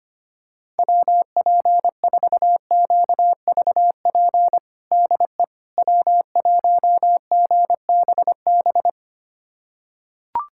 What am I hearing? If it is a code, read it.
Morse code, heard as a keyed tone, about 25 wpm, 700 Hz. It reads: WP4QVP DE W1GBB
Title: USA Callsign DE Callsign 25wpm